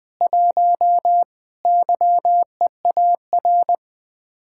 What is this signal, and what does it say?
Morse code, heard as a keyed tone, about 20 wpm, 700 Hz. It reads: 1 YEAR
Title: Speed-Racing Common QSO Phrases Encore - 20wpm